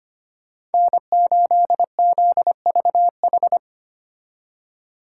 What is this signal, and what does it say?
Morse code, heard as a keyed tone, about 31 wpm, 700 Hz. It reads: N8ZVH